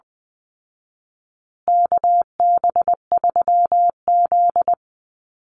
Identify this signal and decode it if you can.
Morse code, heard as a keyed tone, about 19 wpm, 700 Hz. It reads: KB3Z
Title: Call Signs Repeated 2x - 20wpm